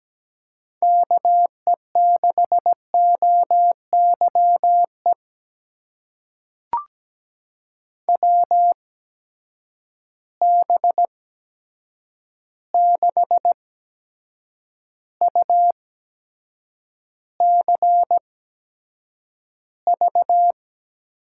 Morse code, heard as a keyed tone, about 17 wpm, spaced as for 7 wpm, 700 Hz. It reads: KE6OYE WB6UCV
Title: USA Call Signs - 6 Characters - 8x Character Spacing - 17wpm